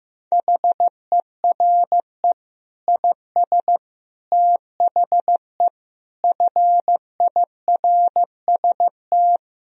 Morse code, heard as a keyed tone, about 15 wpm, 700 Hz. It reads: HERE IS THE FIRST
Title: Sentences from Top 1000 Words - Review Part 2 - 15wpm